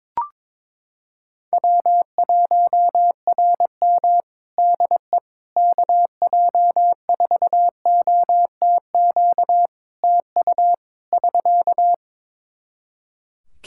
Morse code, heard as a keyed tone, about 22 wpm, 700 Hz. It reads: W1RM DE KJ4OTQ TU <SK>
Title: Practice QSO - Exchange 4 - Ending - 22wpm